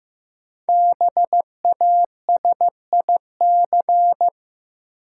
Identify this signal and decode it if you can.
Morse code, heard as a keyed tone, about 15 wpm, 700 Hz. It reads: BASIC
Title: Sentences from Top 1000 Words - Review Part 3 - New Word Drill - 15wpm